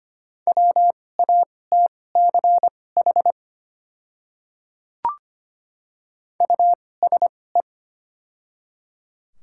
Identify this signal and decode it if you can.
Morse code, heard as a keyed tone, about 25 wpm, spaced as for 18 wpm, 700 Hz. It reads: WATCH USE
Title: 2x Character Spacing - Top 500 Words - 25wpm